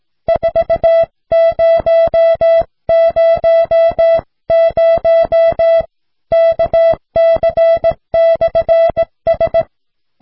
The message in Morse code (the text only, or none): 4000 KC/S